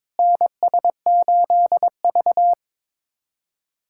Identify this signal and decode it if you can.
Morse code, heard as a keyed tone, about 22 wpm, 700 Hz. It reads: NS8V
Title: USA Callsign DE Callsign - 4 Characters - 22wpm